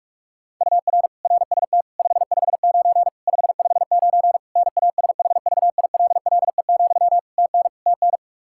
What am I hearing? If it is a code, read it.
Morse code, heard as a keyed tone, about 42 wpm, 700 Hz. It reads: UR RST 559 559 NASHVILLE, TN TN
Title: Practice QSO - Exchange 1 - Beginning - 45wpm